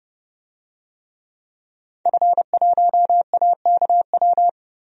Morse code, heard as a keyed tone, about 32 wpm, 700 Hz. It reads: F1AKW
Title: DXCC-100 Call Signs - 5 Characters - Repeated 3x - 30wpm